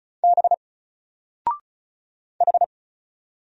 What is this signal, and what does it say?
Morse code, heard as a keyed tone, about 36 wpm, 700 Hz. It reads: B H
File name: Single-Letter-Number-35wpm